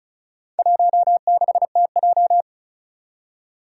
Morse code, heard as a keyed tone, about 34 wpm, 700 Hz. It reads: 16TJ
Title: Four Challenging Letter-Number Combinations 35wpm